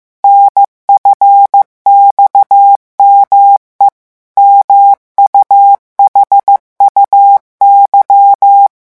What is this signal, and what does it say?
Morse code, heard as a keyed tone, about 15 wpm, 800 Hz. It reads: NFXME MUHUY
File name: morse mp3 / revision7-vitesse 14 mots